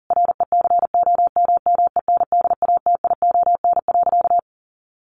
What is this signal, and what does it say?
Morse code, heard as a keyed tone, about 40 wpm, 700 Hz. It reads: RECOMMENDATION.
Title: Sentences from Top 1400 Words 40wpm